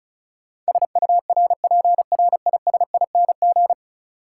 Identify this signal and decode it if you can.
Morse code, heard as a keyed tone, about 34 wpm, 700 Hz. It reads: SURPRISING
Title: Sets of 5 Words - New Word Drill - 35wpm